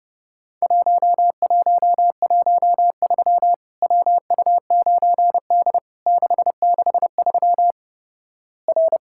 Morse code, heard as a keyed tone, about 30 wpm, 700 Hz. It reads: 1113 WU9D 663 R
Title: Worked All Europe DX Contest - QTC Exchange - 30wpm